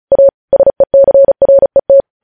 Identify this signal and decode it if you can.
Morse code, heard as a keyed tone, about 36 wpm, 550 Hz. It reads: A SECRET
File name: podcast / News headlines-1x-35x35